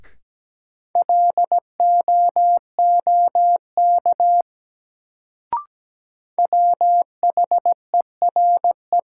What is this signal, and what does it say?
Morse code, heard as a keyed tone, about 17 wpm, 700 Hz. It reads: LOOK WHERE
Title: Sets of 2 Words - New Word Drill - 17wpm